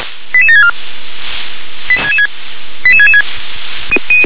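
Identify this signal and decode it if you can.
Five-tone selective call (ZVEI), heard as quick runs of five tones, tones of about 70 ms. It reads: 88764 88666
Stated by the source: ZVEI five-tone call received by SDR.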